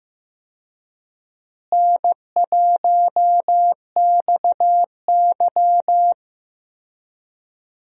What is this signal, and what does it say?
Morse code, heard as a keyed tone, about 15 wpm, 700 Hz. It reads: N1XY
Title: USA Call Signs - 4 Characters - Repeated 3x - 15wpm